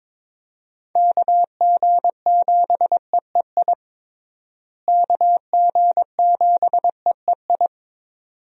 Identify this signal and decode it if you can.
Morse code, heard as a keyed tone, about 22 wpm, 700 Hz. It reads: KG7EEI KG7EEI
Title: USA Call Signs - 6 Characters - Repeated 2x - 22wpm